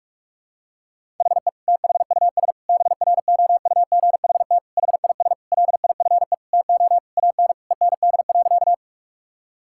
Morse code, heard as a keyed tone, about 46 wpm, 700 Hz. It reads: HE THUS BROUGHT HIS LIFE TO AN END.
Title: Sentences from Top 1000 Words - Review Part 5 - 45wpm